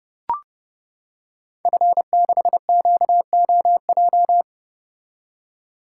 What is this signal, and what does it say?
Morse code, heard as a keyed tone, about 30 wpm, 700 Hz. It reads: F6QOJ
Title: Code Groups - 5 Letter:Numbers - 30wpm